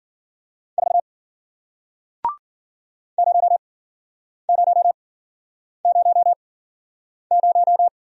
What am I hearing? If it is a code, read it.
Morse code, heard as a keyed tone, about 53 wpm, spaced as for 9 wpm, 700 Hz. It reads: 4 0000